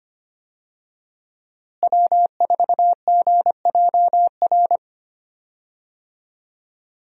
Morse code, heard as a keyed tone, about 25 wpm, 700 Hz. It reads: W4GJR
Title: USA Call Signs - 5 Characters - Repeated 3x - 25wpm